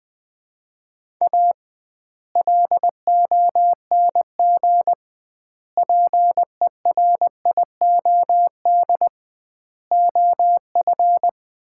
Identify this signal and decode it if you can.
Morse code, heard as a keyed tone, about 20 wpm, 700 Hz. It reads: A LONG PERIOD OF
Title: 2x Word-Spacing - Sets of 5 Words - 20wpm